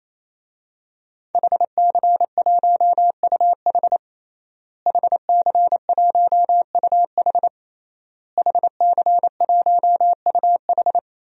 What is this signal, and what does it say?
Morse code, heard as a keyed tone, about 27 wpm, 700 Hz. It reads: HC1UH HC1UH HC1UH
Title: DXCC-100 Call Signs - 5 Characters - Repeated 3x - 28wpm